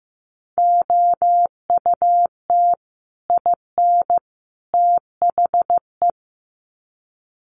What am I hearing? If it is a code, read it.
Morse code, heard as a keyed tone, about 15 wpm, 700 Hz. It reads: OUT IN THE
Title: Sets of 3 Words 15wpm